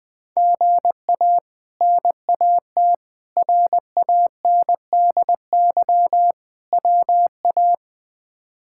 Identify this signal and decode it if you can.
Morse code, heard as a keyed tone, about 21 wpm, 700 Hz. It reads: GA NAT RANDY WA
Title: K1USN SST QSO Practice - 20wpm